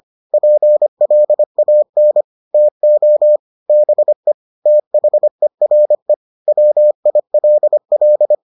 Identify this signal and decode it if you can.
Morse code, heard as a keyed tone, about 25 wpm, 600 Hz. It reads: PLAN TO BE THERE WILL